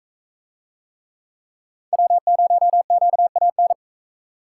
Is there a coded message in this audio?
Yes: Morse code from 1.9 to 3.7 s, heard as a keyed tone, about 39 wpm, 700 Hz. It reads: W0QAN